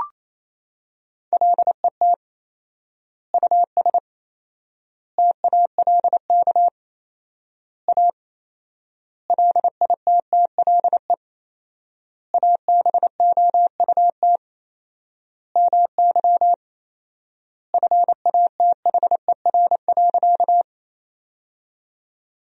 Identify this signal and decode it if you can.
Morse code, heard as a keyed tone, about 28 wpm, 700 Hz. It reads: LET US TALK A LITTLE ABOUT MY FATHER.
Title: 4x Word-Spacing - Sentences from Top 400 Words - 28wpm